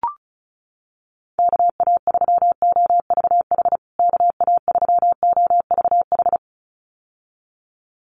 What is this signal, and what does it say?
Morse code, heard as a keyed tone, about 34 wpm, 700 Hz. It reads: KA3OVH KA3OVH
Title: USA Call Signs Repeated 2x 35wpm